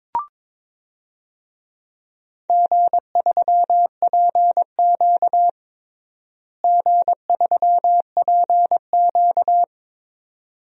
Morse code, heard as a keyed tone, about 22 wpm, 700 Hz. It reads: G3PQ G3PQ
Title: DXCC-100 Call Signs - 4 Characters - Repeated 3x - 22wpm